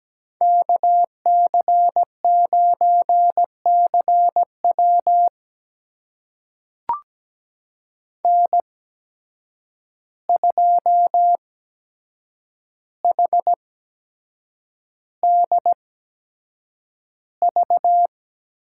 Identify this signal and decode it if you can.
Morse code, heard as a keyed tone, about 17 wpm, spaced as for 7 wpm, 700 Hz. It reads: KC9CW N2HDV
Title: USA Call Signs - 5 Characters - 8x Character Spacing - 17wpm